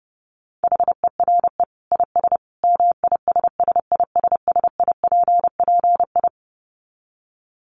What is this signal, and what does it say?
Morse code, heard as a keyed tone, about 28 wpm, 700 Hz. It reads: HERE IS MISSISSIPPI
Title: Dit Challenge 30wpm